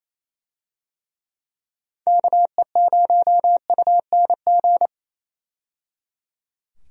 Morse code, heard as a keyed tone, about 28 wpm, 700 Hz. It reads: KE0UNG
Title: USA Call Signs - 6 Characters - 28wpm